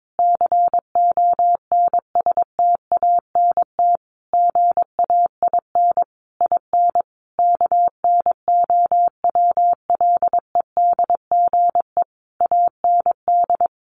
Morse code, heard as a keyed tone, about 22 wpm, 700 Hz. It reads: CONSTANT GAIN IN KNOWLEDGE AND
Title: Sentences from Top 1500 Words 22wpm